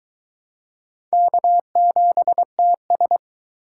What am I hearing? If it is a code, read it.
Morse code, heard as a keyed tone, about 19 wpm, 700 Hz. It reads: K7TS